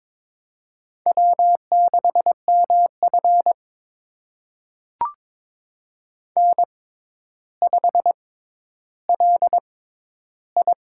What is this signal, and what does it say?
Morse code, heard as a keyed tone, about 23 wpm, spaced as for 11 wpm, 700 Hz. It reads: W6MF N5LI
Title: USA Call Signs - 4 Characters - 6x Character Spacing - 22wpm